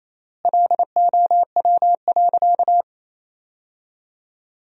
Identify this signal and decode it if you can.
Morse code, heard as a keyed tone, about 28 wpm, 700 Hz. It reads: LOW.
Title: 3x Word-Spacing - Sentences from Top 500 Words - 28wpm